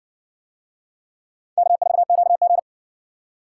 Morse code, heard as a keyed tone, about 60 wpm, 700 Hz. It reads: K4QG